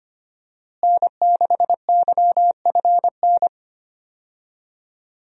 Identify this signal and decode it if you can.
Morse code, heard as a keyed tone, about 25 wpm, 700 Hz. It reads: N6YFN
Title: USA Call Signs - 5 Characters - Repeated 3x - 25wpm